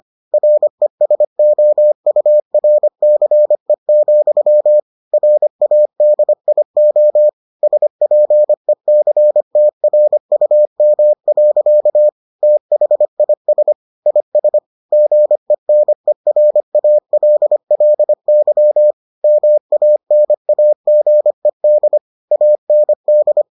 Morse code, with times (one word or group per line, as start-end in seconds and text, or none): 0.3-4.8 RESOURCE,
5.1-7.3 RADIO
7.6-12.1 SPECTRUM.
12.4-13.7 THIS
14.1-14.6 IS
14.9-18.9 GENERALLY
19.2-22.0 MANAGED
22.3-23.4 AND